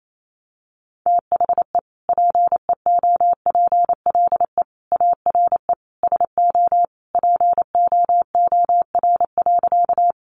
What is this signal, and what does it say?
Morse code, heard as a keyed tone, about 28 wpm, 700 Hz. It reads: THE PEOPLE ARE SO POOR.
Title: Sentences from Top 1000 Words 28wpm